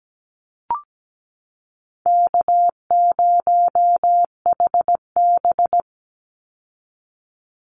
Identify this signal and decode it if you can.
Morse code, heard as a keyed tone, about 17 wpm, 700 Hz. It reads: K0HB
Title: Easy Call Signs Repeated 3x - 17wpm